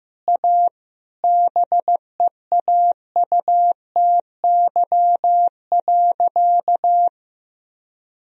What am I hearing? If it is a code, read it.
Morse code, heard as a keyed tone, about 15 wpm, 700 Hz. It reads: A BEAUTY.
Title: Sentences from Top 1000 Words - Review Part 2 - 15wpm